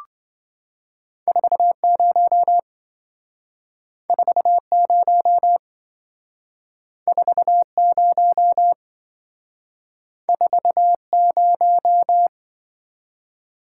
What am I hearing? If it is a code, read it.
Morse code, heard as a keyed tone, about 26 wpm, 700 Hz. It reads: 40 40 40 40